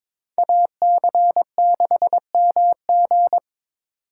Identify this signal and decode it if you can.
Morse code, heard as a keyed tone, about 21 wpm, 700 Hz. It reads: AC6MG
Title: USA Call Signs - 4x Character Spacing - 22wpm